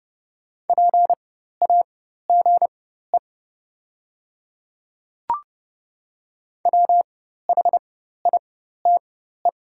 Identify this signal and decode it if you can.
Morse code, heard as a keyed tone, about 30 wpm, spaced as for 14 wpm, 700 Hz. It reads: PAGE WHITE
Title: 4x Character Spacing - Top 300 Words - 30wpm